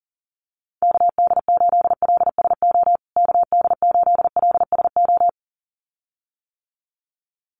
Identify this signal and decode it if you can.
Morse code, heard as a keyed tone, about 40 wpm, 700 Hz. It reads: KD8LSO KD8LSO
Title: USA Call Signs Repeated 2x 40wpm